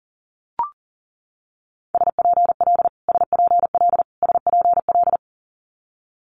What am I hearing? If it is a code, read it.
Morse code, heard as a keyed tone, about 40 wpm, 700 Hz. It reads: SPL SPL SPL
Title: Multi-letter Phonemes Repeated 3x 40wpm